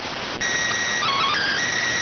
SSTV (slow-scan television), PD 290.